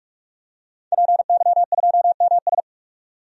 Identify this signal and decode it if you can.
Morse code, heard as a keyed tone, about 46 wpm, 700 Hz. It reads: PY2MS
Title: DXCC-100 Call Signs - 5 Characters - 45wpm